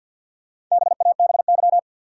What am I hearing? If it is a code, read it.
Morse code, heard as a keyed tone, about 45 wpm, 700 Hz. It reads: BABY
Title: Sentences from Top 1300 Words - New Word Drill - 50wpm